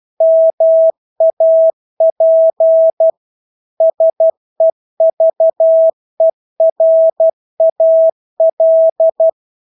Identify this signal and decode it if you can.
Morse code, heard as a keyed tone, about 12 wpm, 650 Hz. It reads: MAP SEVERAL
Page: Engelske ord 12 wpm